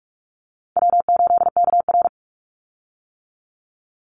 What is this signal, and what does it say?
Morse code, heard as a keyed tone, about 44 wpm, 700 Hz. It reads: W8KR
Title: Call Signs - 45wpm